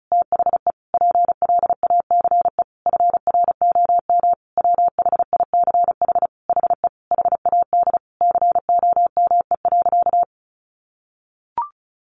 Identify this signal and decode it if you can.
Morse code, heard as a keyed tone, about 34 wpm, 700 Hz. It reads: THE PLACE FROM WHICH HE HAD COME.
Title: Sentences from Top 1500 Words 35wpm